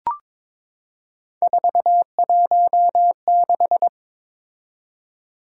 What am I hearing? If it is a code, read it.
Morse code, heard as a keyed tone, about 22 wpm, 700 Hz. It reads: 416
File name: practice / 22wpm / Three Letter-Number Combinations - 22wpm